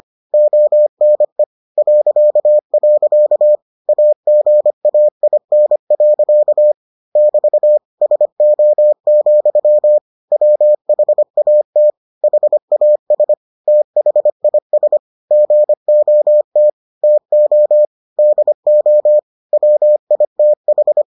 Morse code, heard as a keyed tone, about 25 wpm, 600 Hz. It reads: ONE .. AGAIN. <BT> SO, WHAT HAS THIS GOT TO DO WITH